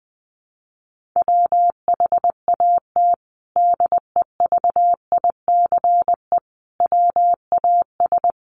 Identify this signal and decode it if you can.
Morse code, heard as a keyed tone, about 20 wpm, 700 Hz. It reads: WHAT DEVICE WAS
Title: Sentences from Top 1500 Words 20wpm